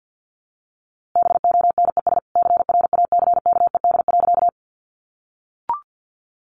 Morse code, heard as a keyed tone, about 48 wpm, 700 Hz. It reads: BONES CRACKED.
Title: A Sound Of Thunder 50wpm